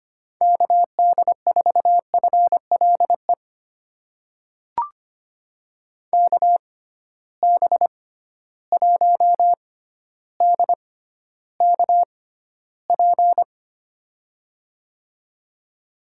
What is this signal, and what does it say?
Morse code, heard as a keyed tone, about 25 wpm, spaced as for 13 wpm, 700 Hz. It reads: KD4FLE KB1DKP
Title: USA Call Signs - 6x Character Spacing - 25wpm